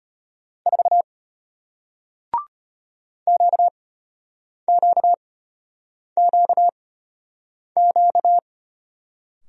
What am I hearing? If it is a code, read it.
Morse code, heard as a keyed tone, about 33 wpm, spaced as for 8 wpm, 700 Hz. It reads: 4 QQQQ